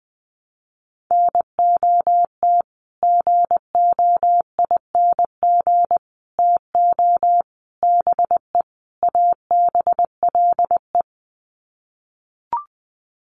Morse code, heard as a keyed tone, about 20 wpm, 700 Hz. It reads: NOT GOING TO BE ABLE
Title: Sets of 5 Words 20wpm